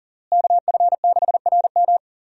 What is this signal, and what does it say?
Morse code, heard as a keyed tone, about 40 wpm, 700 Hz. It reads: KF6RM
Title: USA Call Signs - 5 Characters - Repeated 3x - 40wpm